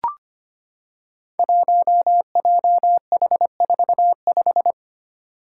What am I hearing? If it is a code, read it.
Morse code, heard as a keyed tone, about 25 wpm, 700 Hz. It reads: 1JH45
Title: Five Challenging Letter-Number Combinations 25wpm